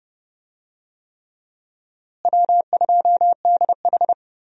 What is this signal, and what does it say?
Morse code, heard as a keyed tone, about 32 wpm, 700 Hz. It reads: W2DH
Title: USA Call Signs - 4 Characters - Repeated 3x - 30wpm